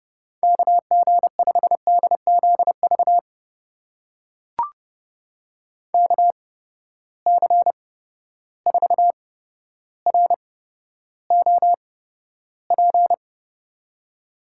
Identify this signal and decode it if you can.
Morse code, heard as a keyed tone, about 30 wpm, spaced as for 13 wpm, 700 Hz. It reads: KG5DZV KC4ROP
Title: USA Call Signs - 6 Characters - 8x Character Spacing - 30wpm